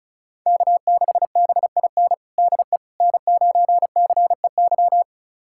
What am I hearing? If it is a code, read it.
Morse code, heard as a keyed tone, about 34 wpm, 700 Hz. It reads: K6BIN DE N9CEY
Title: USA Callsign DE Callsign - 5 Characters - 35wpm